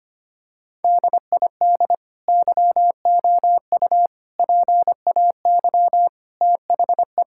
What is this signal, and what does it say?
Morse code, heard as a keyed tone, about 25 wpm, 700 Hz. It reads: DID YOU PAY THE
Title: Sentences from Top 1000 Words - Review Part 5 - 25wpm